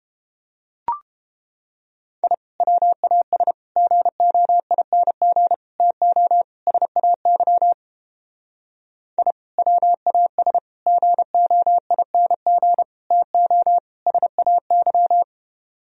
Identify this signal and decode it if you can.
Morse code, heard as a keyed tone, about 29 wpm, 700 Hz. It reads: I WAS GOING TO SAY I WAS GOING TO SAY